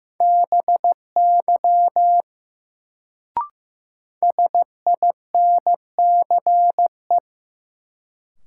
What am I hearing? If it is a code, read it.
Morse code, heard as a keyed tone, about 15 wpm, 700 Hz. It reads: BY SINCE
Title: Sets of 2 Words - New Word Drill - 15wpm